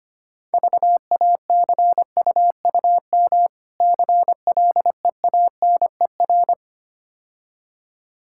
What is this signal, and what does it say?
Morse code, heard as a keyed tone, about 26 wpm, 700 Hz. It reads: VACUUM CLEANER
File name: practice / 25wpm / Difficult Words - Speed-Racing with Hints - 25wpm